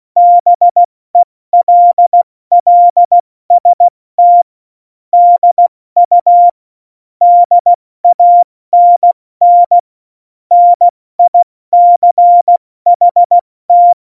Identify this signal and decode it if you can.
Morse code, heard as a keyed tone, about 16 wpm, 700 Hz. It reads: BELLST DU DANN NICHT
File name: Morsekurs / Witze / Witz 12. Juni 2018, 70 F80